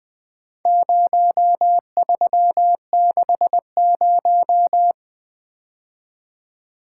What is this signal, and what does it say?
Morse code, heard as a keyed tone, about 21 wpm, 700 Hz. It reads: 0360
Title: Speed-Racing - Four Letter-Number Combinations - 20wpm